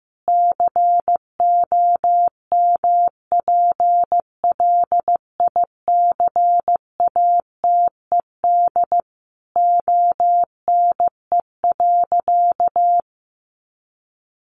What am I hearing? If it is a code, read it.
Morse code, heard as a keyed tone, about 14 wpm, 700 Hz. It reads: COMPLICATED ONE.
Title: Sentences from Top 1200 Words 15wpm